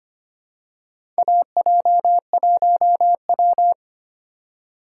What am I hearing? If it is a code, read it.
Morse code, heard as a keyed tone, about 25 wpm, 700 Hz. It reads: AJ1W
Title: USA Call Signs - 4 Characters - 2x Character Spacing - 25wpm